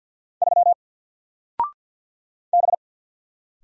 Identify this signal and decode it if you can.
Morse code, heard as a keyed tone, about 50 wpm, 700 Hz. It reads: 3 B
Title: Single Letter-Number 50wpm